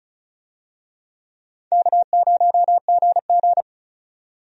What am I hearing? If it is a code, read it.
Morse code, heard as a keyed tone, about 34 wpm, 700 Hz. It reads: K0GG
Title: USA Call Signs - 4 Characters - 35wpm